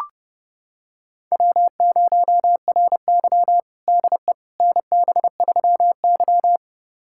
Morse code, heard as a keyed tone, about 32 wpm, 700 Hz. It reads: W0RY DE NB3Y
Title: USA Callsign DE Callsign - 4 Characters - 30wpm